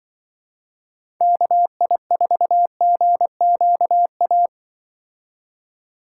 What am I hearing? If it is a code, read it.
Morse code, heard as a keyed tone, about 25 wpm, 700 Hz. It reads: KI4GQA
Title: Speed Racing - USA 2x3 Call Signs 20wpm